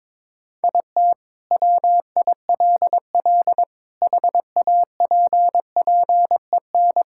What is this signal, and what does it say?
Morse code, heard as a keyed tone, about 22 wpm, 700 Hz. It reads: IT WILL HAPPEN
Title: Sentences from Top 1000 Words - Review Part 5 - 22wpm